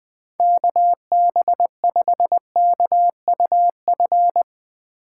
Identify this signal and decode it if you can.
Morse code, heard as a keyed tone, about 22 wpm, 700 Hz. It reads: KB5KUF
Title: Speed Racing - USA 2x3 Call Signs 20wpm